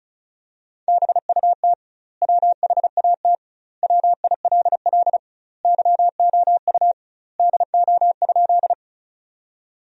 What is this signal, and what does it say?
Morse code, heard as a keyed tone, about 35 wpm, 700 Hz. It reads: BUT WHAT WILL YOU DO?
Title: 2x Word-Spacing - Sentences from Top 100 Words - 35wpm